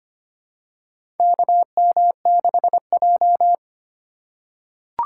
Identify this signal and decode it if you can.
Morse code, heard as a keyed tone, about 25 wpm, 700 Hz. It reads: KM6J
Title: USA Call Signs - 4 Characters - 2x Character Spacing - 25wpm